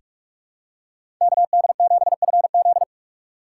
Morse code, heard as a keyed tone, about 46 wpm, 700 Hz. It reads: KD7FZ
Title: USA Call Signs - 3x Character Spacing - 45wpm